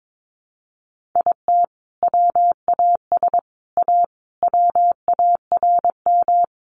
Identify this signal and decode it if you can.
Morse code, heard as a keyed tone, about 22 wpm, 700 Hz. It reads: IT WAS A WARM
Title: Sentences from Top 1500 Words 22wpm